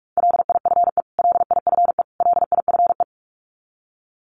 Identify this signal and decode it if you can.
Morse code, heard as a keyed tone, about 44 wpm, 700 Hz. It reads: LIFE LIFE LIFE
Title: Top 500 Words Repeated 45wpm